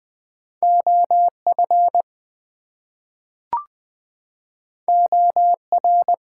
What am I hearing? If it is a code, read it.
Morse code, heard as a keyed tone, about 20 wpm, 700 Hz. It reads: OF OR
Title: Two Letter Words - 20wpm